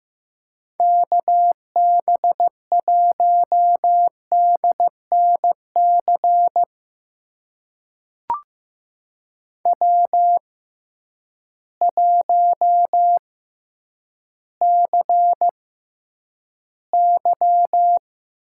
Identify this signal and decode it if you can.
Morse code, heard as a keyed tone, about 15 wpm, spaced as for 9 wpm, 700 Hz. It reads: KB1DNC W1CY
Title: USA Call Signs - 6x Character Spacing - 15wpm